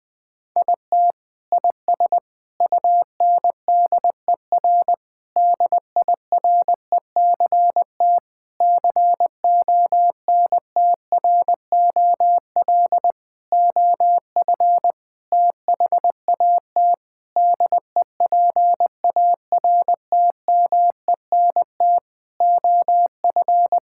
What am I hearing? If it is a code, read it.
Morse code, heard as a keyed tone, about 20 wpm, 700 Hz. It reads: IT IS UNDER DIRECT CONTROL OF THAT DEPARTMENT OF
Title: Sentences from Top 1000 Words - Review Part 5 - 20wpm